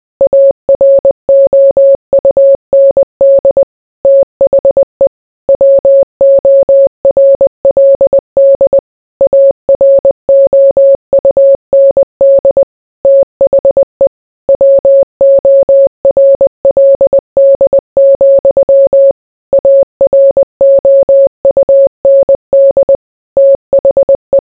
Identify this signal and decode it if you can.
Morse code, heard as a keyed tone, about 20 wpm, 550 Hz. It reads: AROUND THE WORLD AROUND THE WORLD, AROUND THE